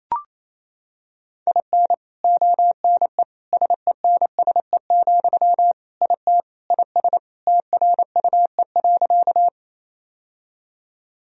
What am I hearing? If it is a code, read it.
Morse code, heard as a keyed tone, about 28 wpm, 700 Hz. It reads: IN ONE SENSE, IT IS TRUE.
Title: Sentences from Top 1000 Words - Review Part 4 - 28wpm